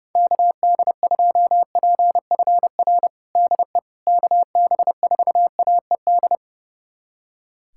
Morse code, heard as a keyed tone, about 30 wpm, 700 Hz. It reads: KD2PFR DE KB4AED
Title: USA Callsign DE Callsign - 6 Characters - 30wpm